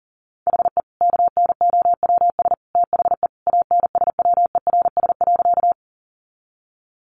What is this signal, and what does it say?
Morse code, heard as a keyed tone, about 40 wpm, 700 Hz. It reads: HE KNOWS THE ANSWERS.
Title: Additional Easy Phrases 40wpm